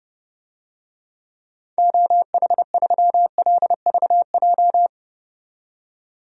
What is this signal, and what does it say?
Morse code, heard as a keyed tone, about 32 wpm, 700 Hz. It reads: OH3LVJ
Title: DXCC-100 Call Signs - 6 Characters - 30wpm